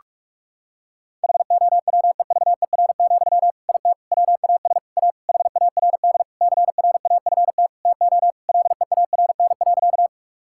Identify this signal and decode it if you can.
Morse code, heard as a keyed tone, about 46 wpm, 700 Hz. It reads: HOWEVER, IT WAS A HARD CRAFT TO LEARN.
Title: Sentences from Top 1000 Words - Review Part 6 - 45wpm